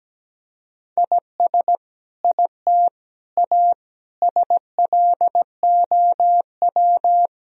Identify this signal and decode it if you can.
Morse code, heard as a keyed tone, about 17 wpm, 700 Hz. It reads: IS IT A SLOW
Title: Sentences from Top 1000 Words - Review Part 2 - 17wpm